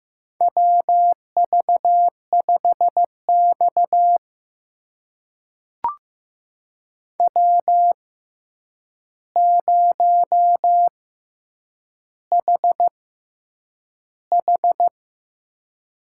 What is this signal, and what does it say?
Morse code, heard as a keyed tone, about 15 wpm, spaced as for 8 wpm, 700 Hz. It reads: WV5X W0HH
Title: USA Call Signs - 6x Character Spacing - 15wpm